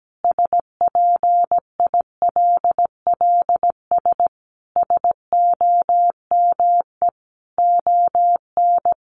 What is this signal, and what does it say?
Morse code, heard as a keyed tone, about 17 wpm, 700 Hz. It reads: SPILLS SOME ON
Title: Additional Easy Phrases 17wpm